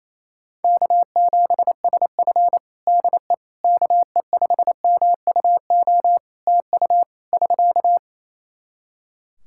Morse code, heard as a keyed tone, about 28 wpm, 700 Hz. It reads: K7SF DE KE5MUO TU <SK>
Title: K1USN SST QSO Practice 28wpm